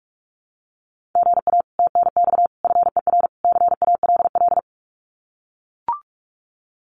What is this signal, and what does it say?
Morse code, heard as a keyed tone, about 44 wpm, 700 Hz. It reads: GA TNX FER CALL
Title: Common QSO Phrases Encore 45wpm